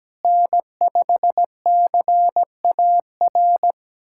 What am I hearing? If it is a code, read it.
Morse code, heard as a keyed tone, about 17 wpm, 700 Hz. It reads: N5CAR
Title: USA Callsign DE Callsign 17wpm